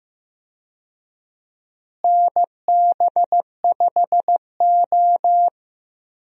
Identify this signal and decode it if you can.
Morse code, heard as a keyed tone, about 15 wpm, 700 Hz. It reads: NB5O
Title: USA Call Signs - 4 Characters - Repeated 3x - 15wpm